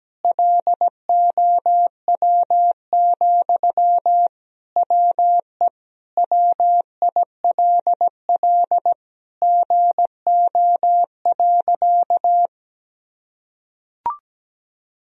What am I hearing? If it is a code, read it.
Morse code, heard as a keyed tone, about 17 wpm, 700 Hz. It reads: LOW, WE WILL GO.
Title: Sentences from Top 1000 Words - Review Part 1 - 17wpm